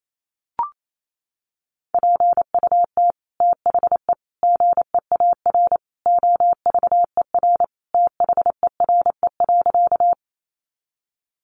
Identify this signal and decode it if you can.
Morse code, heard as a keyed tone, about 28 wpm, 700 Hz. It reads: PUT THE GEAR OVER THERE.
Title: Sentences from Top 1400 Words 28wpm